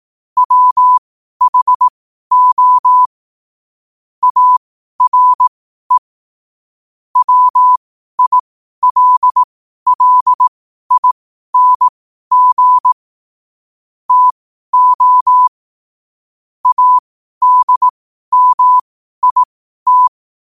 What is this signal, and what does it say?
Morse code, heard as a keyed tone, about 18 wpm, spaced as for 12 wpm, 1000 Hz. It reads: WHO ARE WILLING TO ADMIT
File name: qotdfiles / 12wpm